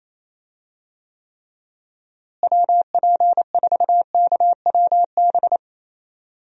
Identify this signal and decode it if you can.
Morse code, heard as a keyed tone, about 28 wpm, 700 Hz. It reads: WP4KWB
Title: USA Call Signs - 6 Characters - Repeated 3x - 28wpm